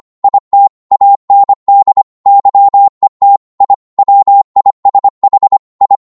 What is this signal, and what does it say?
Morse code, heard as a keyed tone, about 25 wpm, 800 Hz. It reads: IT AND YET I WISH I